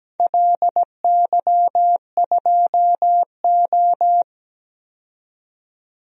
Morse code, heard as a keyed tone, about 17 wpm, 700 Hz. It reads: LY2O
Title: DXCC-100 Call Signs - 4 Characters - 17wpm